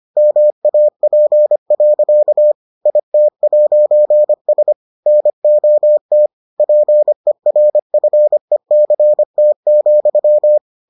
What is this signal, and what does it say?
Morse code, heard as a keyed tone, about 25 wpm, 600 Hz. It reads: MAP. IT'S NOT PERFECT,